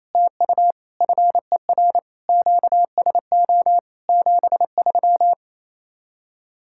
Morse code, heard as a keyed tone, about 37 wpm, 700 Hz. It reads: TU FER QSO 73